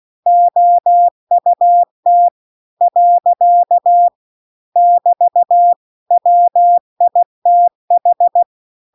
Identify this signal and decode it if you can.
Morse code, heard as a keyed tone, about 16 wpm, 700 Hz. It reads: OUT . <BT> WITH